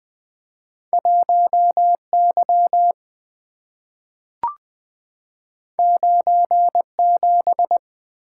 Morse code, heard as a keyed tone, about 20 wpm, 700 Hz. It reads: 1 Y 9 7
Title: Two Letter-Number Combinations - 20wpm